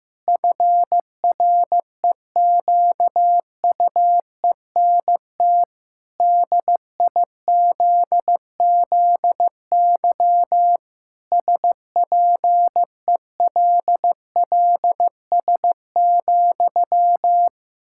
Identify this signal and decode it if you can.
Morse code, heard as a keyed tone, about 15 wpm, 700 Hz. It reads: FREQUENT DIZZY SPELLS,
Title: Pangram Challenge v2 - 15wpm